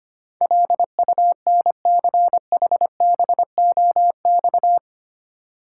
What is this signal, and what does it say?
Morse code, heard as a keyed tone, about 25 wpm, 690 Hz. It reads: LUNCHBOX
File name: practice / 25wpm / Easy Words with Hints - 25wpm